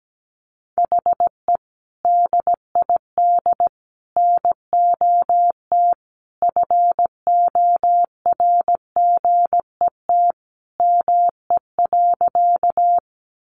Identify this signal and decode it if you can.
Morse code, heard as a keyed tone, about 17 wpm, 700 Hz. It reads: HE DID NOT FORGET ME.
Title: Sentences from Top 1200 Words 17wpm